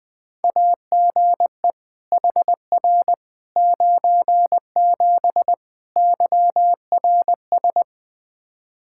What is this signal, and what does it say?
Morse code, heard as a keyed tone, about 20 wpm, 700 Hz. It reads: AGE HR 97 YRS
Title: Practice QSO - Exchange 3 - More Info - 20wpm